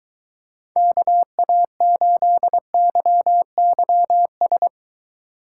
Morse code, heard as a keyed tone, about 23 wpm, 700 Hz. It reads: KA8YYS